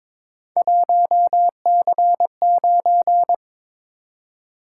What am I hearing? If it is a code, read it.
Morse code, heard as a keyed tone, about 22 wpm, 700 Hz. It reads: 1C9
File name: practice / 22wpm / Three Letter-Number Combinations - 22wpm